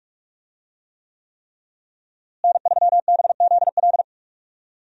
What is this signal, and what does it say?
Morse code, heard as a keyed tone, about 46 wpm, 700 Hz. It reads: N3BZL
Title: USA Call Signs - 5 Characters - Repeated 3x - 45wpm